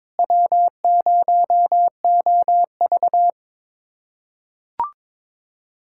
Morse code, heard as a keyed tone, about 22 wpm, 700 Hz. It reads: W0OV
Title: USA Callsign DE Callsign - 4 Characters - 22wpm